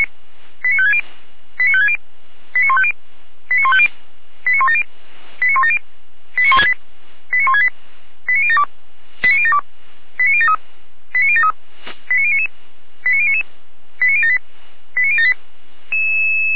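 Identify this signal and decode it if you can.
Five-tone selective call (ZVEI), heard as quick runs of five tones, tones of about 70 ms. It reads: 89560 89560 89150 89150 89170 89170 89168 89168 89072 89072 89073 89073 89099 89099 89078 89078
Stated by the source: ZVEI five-tone call 89167 received by SDR, channel 455, recorded by Feuerwehr Orlamünde.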